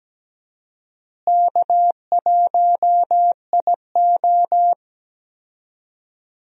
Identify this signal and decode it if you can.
Morse code, heard as a keyed tone, about 17 wpm, 700 Hz. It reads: K1IO
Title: USA Call Signs - 4 Characters - Repeated 2x - 17wpm